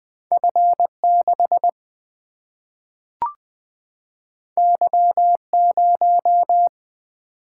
Morse code, heard as a keyed tone, about 20 wpm, 700 Hz. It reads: F6 Y0
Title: Single-Letter Single-Number 20wpm